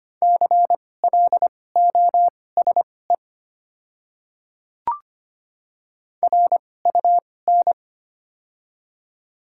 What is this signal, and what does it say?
Morse code, heard as a keyed tone, about 25 wpm, spaced as for 18 wpm, 700 Hz. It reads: CLOSE RUN
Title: 2x Character Spacing - Top 300 Words - 25wpm